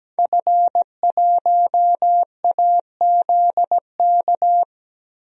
Morse code, heard as a keyed tone, about 17 wpm, 700 Hz. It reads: F1AZK
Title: DXCC-100 Call Signs - 5 Characters - Repeated 2x - 17wpm